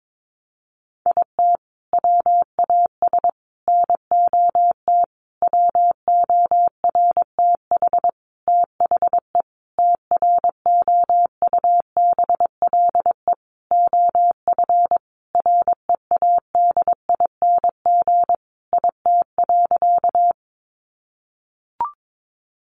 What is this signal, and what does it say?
Morse code, heard as a keyed tone, about 22 wpm, 700 Hz. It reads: IT WAS NOT WORTH THE TROUBLE OF READING IT.
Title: Sentences from Top 1200 Words 22wpm